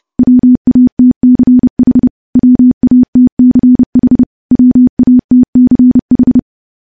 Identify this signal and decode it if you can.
Morse code, heard as a keyed tone, about 30 wpm, 250 Hz. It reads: WATCH WATCH WATCH
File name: morse-training-30WPM